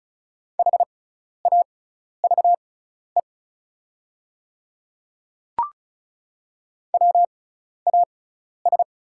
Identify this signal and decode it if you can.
Morse code, heard as a keyed tone, about 36 wpm, spaced as for 12 wpm, 690 Hz. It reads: HAVE WAS
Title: 6x Character Spacing - Top 200 Words - 35wpm